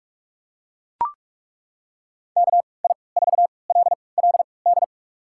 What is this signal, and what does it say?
Morse code, heard as a keyed tone, about 41 wpm, spaced as for 25 wpm, 700 Hz. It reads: KI4LLD
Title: USA Call Signs - 6 Characters - 3x Character Spacing - 45wpm